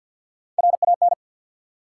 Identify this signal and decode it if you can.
Morse code, heard as a keyed tone, about 47 wpm, 700 Hz. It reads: RAN
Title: Sentences from Top 500 Words - New Word Drill - 50wpm